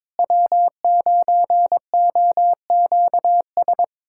Morse code, heard as a keyed tone, about 21 wpm, 700 Hz. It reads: W9OQS
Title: USA Callsign DE Callsign - 5 Characters - 22wpm